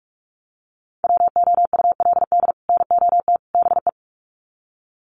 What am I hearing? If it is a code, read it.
Morse code, heard as a keyed tone, about 44 wpm, 700 Hz. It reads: WOULD NOT BE
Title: Sets of 3 Words 45wpm